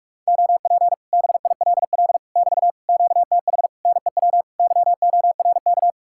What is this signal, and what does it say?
Morse code, heard as a keyed tone, about 56 wpm, 700 Hz. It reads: OP BILL <BT> QTH NEW YORK
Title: Speed-Racing Common QSO Phrases Encore - 45wpm